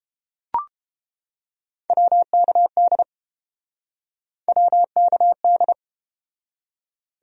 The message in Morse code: WKD WKD

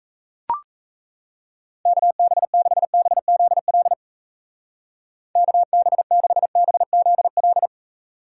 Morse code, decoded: KB6BZL KB6BZL

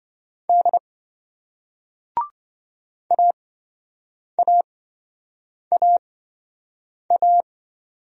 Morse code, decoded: D AAAA

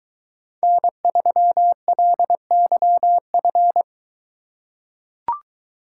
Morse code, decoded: N3LYF

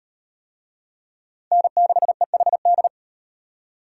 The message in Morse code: N6EHD

